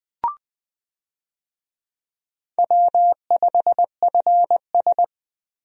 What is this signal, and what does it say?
Morse code, heard as a keyed tone, about 20 wpm, 700 Hz. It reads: W5FS
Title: USA Call Signs - 4 Characters - 20wpm